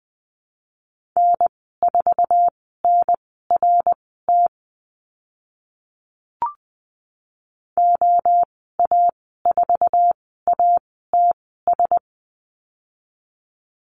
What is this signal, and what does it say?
Morse code, heard as a keyed tone, about 20 wpm, spaced as for 14 wpm, 700 Hz. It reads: N4NRT OA4ATS
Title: 069 Lesson 12g - Letter d - Call Signs 20wpm